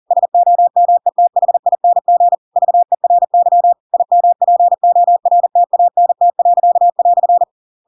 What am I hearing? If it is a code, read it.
Morse code, heard as a keyed tone, about 40 wpm, 700 Hz. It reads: SOMETHING VERY IMPORTANT."
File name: Pinocchio / Pinocchio ch8 40 40 ik1hge